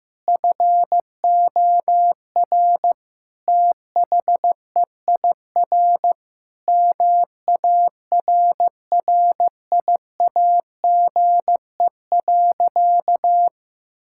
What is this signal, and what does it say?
Morse code, heard as a keyed tone, about 15 wpm, 700 Hz. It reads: FOR THEIR MARRIAGE.
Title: Sentences from Top 1000 Words - Review Part 6 - 15wpm